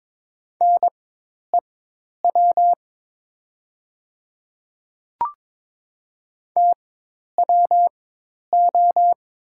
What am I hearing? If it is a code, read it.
Morse code, heard as a keyed tone, about 22 wpm, spaced as for 10 wpm, 690 Hz. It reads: NEW TWO
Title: 4x Character Spacing - Top 300 Words - 22wpm